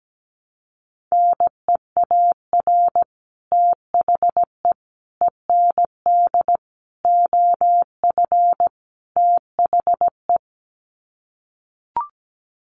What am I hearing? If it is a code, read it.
Morse code, heard as a keyed tone, about 17 wpm, 700 Hz. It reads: NEAR THE END OF THE
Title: Sets of 5 Words Encore 17wpm